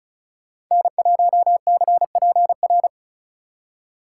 Morse code, heard as a keyed tone, about 42 wpm, 700 Hz. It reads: N1CPR
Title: Speed Racing - USA 1x3 Call Signs 35wpm